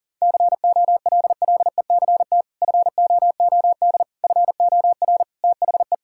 Morse code, heard as a keyed tone, about 40 wpm, 700 Hz. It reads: COLLECT FOOD FOR THE
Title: Sentences from Top 1600 Words 40wpm